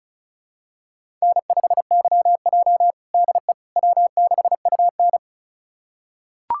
Morse code, heard as a keyed tone, about 36 wpm, 700 Hz. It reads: N5YJ DE W6UN
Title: USA Callsign DE Callsign - 4 Characters - 35wpm